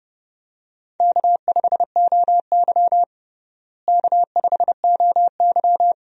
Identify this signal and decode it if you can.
Morse code, heard as a keyed tone, about 30 wpm, 700 Hz. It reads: K5OY K5OY
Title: USA Call Signs - 4 Characters - Repeated 2x - 30wpm